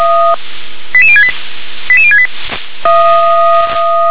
Five-tone selective call (ZVEI), heard as quick runs of five tones, tones of about 70 ms. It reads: 88068 88068
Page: /357/SDR/ZVEI/88159/